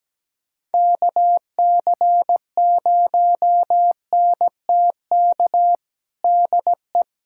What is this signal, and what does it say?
Morse code, heard as a keyed tone, about 17 wpm, 700 Hz. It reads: KC0NTK DE
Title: USA Callsign DE Callsign 17wpm